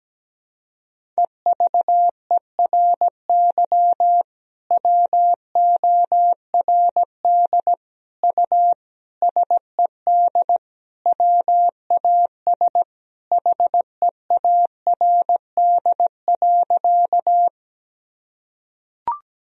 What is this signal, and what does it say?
Morse code, heard as a keyed tone, about 17 wpm, 700 Hz. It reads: EVERY WORD U SED WAS HEARD.
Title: Sentences from Top 400 Words and Abbreviations 17wpm